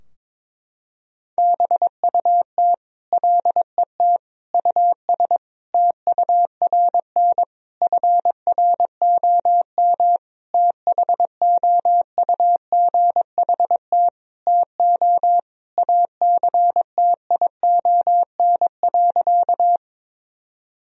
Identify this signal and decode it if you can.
Morse code, heard as a keyed tone, about 22 wpm, 700 Hz. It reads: BUT LET US TURN FROM THOUGHT TO ACTION.
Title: Sentences from Top 1000 Words - Review Part 4 - 22wpm